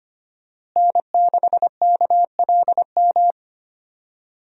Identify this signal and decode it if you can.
Morse code, heard as a keyed tone, about 25 wpm, 700 Hz. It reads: N6KLM
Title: USA Call Signs - 5 Characters - Repeated 2x - 25wpm